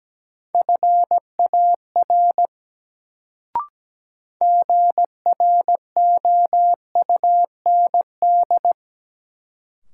Morse code, heard as a keyed tone, about 17 wpm, 700 Hz. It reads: FAR GROUND
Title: Sets of 3 Words - New Word Drill - 17wpm